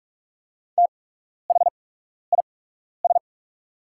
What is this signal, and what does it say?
Morse code, heard as a keyed tone, about 44 wpm, spaced as for 12 wpm, 700 Hz. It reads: THIS
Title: 8x Character Spacing - Top 100 Words - 45wpm